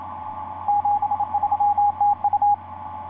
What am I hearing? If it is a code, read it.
Morse code, heard as a keyed tone, about 27 wpm, 820 Hz. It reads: 73 TU
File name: 9x0lx sp1mwk etc 12m cw 12feb2010 1146